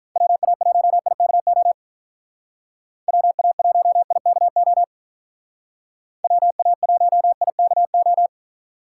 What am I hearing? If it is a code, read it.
Morse code, heard as a keyed tone, about 43 wpm, 700 Hz. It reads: WA1IKO WA1IKO WA1IKO
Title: Speed Racing - USA 2x3 Call Signs 35wpm